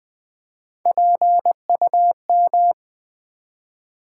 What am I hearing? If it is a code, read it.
Morse code, heard as a keyed tone, about 20 wpm, 700 Hz. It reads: PUM
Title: Build A Word - 1 Letter At A Time - 3 to 5-Letter Words - 20wpm